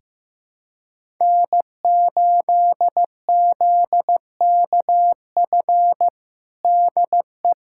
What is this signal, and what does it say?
Morse code, heard as a keyed tone, about 15 wpm, 700 Hz. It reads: N8ZKF DE
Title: USA Callsign DE Callsign - 5 Characters - 15wpm